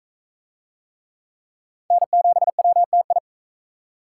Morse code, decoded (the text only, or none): N7WTI